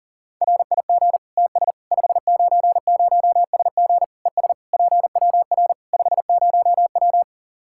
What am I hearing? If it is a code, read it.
Morse code, heard as a keyed tone, about 37 wpm, 700 Hz. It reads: RIG TS 590SG ES PWR 50W
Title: Practice QSO - Exchange 2 - Info - 40wpm